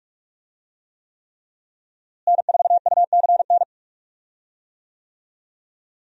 Morse code, heard as a keyed tone, about 43 wpm, 700 Hz. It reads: N4UCN
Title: USA Call Signs - 5 Characters - Repeated 3x - 45wpm